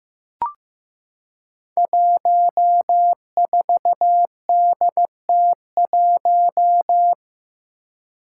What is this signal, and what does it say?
Morse code, heard as a keyed tone, about 15 wpm, 700 Hz. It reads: 14DT1
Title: Five Challenging Letter-Number Combinations 15wpm